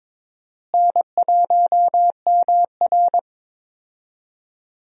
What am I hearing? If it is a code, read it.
Morse code, heard as a keyed tone, about 22 wpm, 700 Hz. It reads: N1MR
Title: USA Call Signs - 4 Characters - Repeated 2x - 22wpm